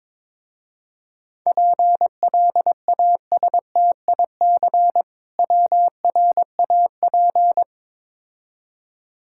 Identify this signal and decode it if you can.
Morse code, heard as a keyed tone, about 22 wpm, 700 Hz. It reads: PLASTIC WRAP
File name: Easy Words - Speed-Racing with Hints - 22wpm